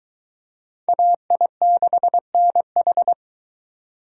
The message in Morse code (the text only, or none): AI6NH